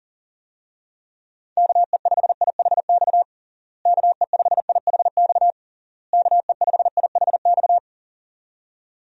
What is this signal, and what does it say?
Morse code, heard as a keyed tone, about 37 wpm, 700 Hz. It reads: KE5IHX KE5IHX KE5IHX
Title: USA Call Signs - 6 Characters - Repeated 3x - 40wpm